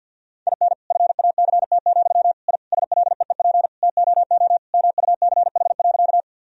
Morse code, heard as a keyed tone, about 51 wpm, 700 Hz. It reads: IN FACT, I SLEEP TOO MUCH.
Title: Sentences from Top 1000 Words - Review Part 3 - 50wpm